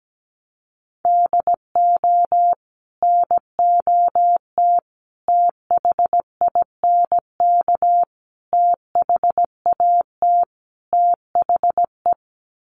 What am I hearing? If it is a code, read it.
Morse code, heard as a keyed tone, about 17 wpm, 700 Hz. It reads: DO NOT THINK THAT THE
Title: Sets of 5 Words Encore2 17wpm